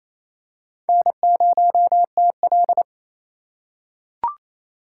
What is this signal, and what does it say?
Morse code, heard as a keyed tone, about 28 wpm, 700 Hz. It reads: N0TL
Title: USA Call Signs - 4 Characters - 3x Character Spacing - 28wpm